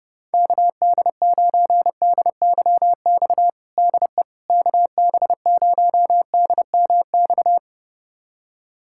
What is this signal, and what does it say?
Morse code, heard as a keyed tone, about 30 wpm, 700 Hz. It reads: KD9DYX DE KB0DMX
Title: USA Callsign DE Callsign 30wpm